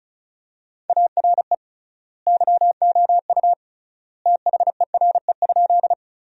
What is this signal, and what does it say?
Morse code, heard as a keyed tone, about 34 wpm, 700 Hz. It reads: ARE YOU THERE?
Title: 3x Word-Spacing - Sentences from Top 100 Words - 35wpm